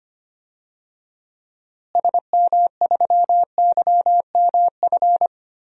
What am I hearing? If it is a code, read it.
Morse code, heard as a keyed tone, about 25 wpm, 700 Hz. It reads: SM3YMF
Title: DXCC-100 Call Signs - 6 Characters - Repeated 2x - 25wpm